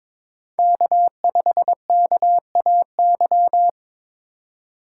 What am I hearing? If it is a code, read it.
Morse code, heard as a keyed tone, about 22 wpm, 700 Hz. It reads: K5KAY
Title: USA Call Signs - 5 Characters - 4x Character Spacing - 22wpm